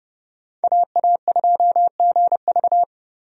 Morse code, heard as a keyed tone, about 29 wpm, 700 Hz. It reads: AA2GV
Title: USA Call Signs - 5 Characters - 30wpm